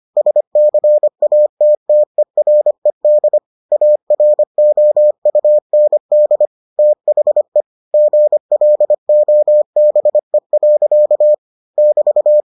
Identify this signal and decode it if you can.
Morse code, heard as a keyed tone, about 25 wpm, 600 Hz. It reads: SCATTERED AROUND THE GLOBE. <BT>